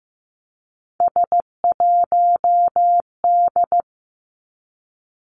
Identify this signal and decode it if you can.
Morse code, heard as a keyed tone, about 15 wpm, 700 Hz. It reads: S1D
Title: Three Challenging Letter-Number Combinations 15wpm